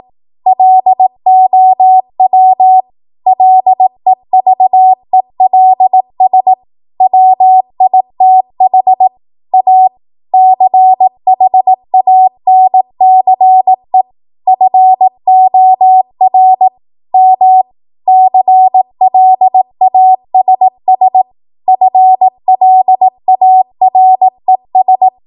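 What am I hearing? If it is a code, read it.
Morse code, heard as a keyed tone, about 18 wpm, 750 Hz. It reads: LOW LEVELS WITH A CHANCE FOR M CLASS FLARES
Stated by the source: ARRL 18 wpm Morse code practice archive transmission.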